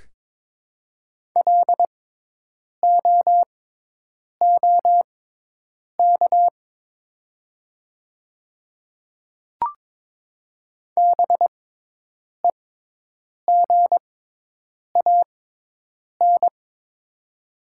Morse code, heard as a keyed tone, about 22 wpm, spaced as for 8 wpm, 690 Hz. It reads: LOOK BEGAN
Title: 6x Character Spacing - Top 300 Words - 22wpm